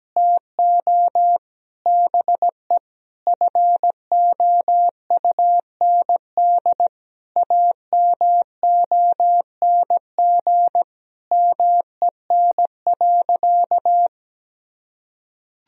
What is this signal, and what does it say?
Morse code, heard as a keyed tone, about 17 wpm, 700 Hz. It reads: TO BE FOUND AMONG MEN.
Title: Sentences from Top 1000 Words - Review Part 3 - 17wpm